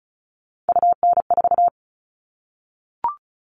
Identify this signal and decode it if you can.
Morse code, heard as a keyed tone, about 34 wpm, 700 Hz. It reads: UN4
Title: All Cut Numbers - 3-Digit 35wpm